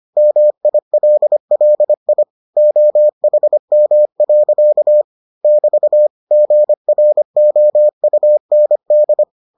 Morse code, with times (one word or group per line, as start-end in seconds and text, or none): 0.2-2.2 MILLI
2.6-5.0 OHM.
5.4-6.1 <BT>
6.3-9.2 GROUND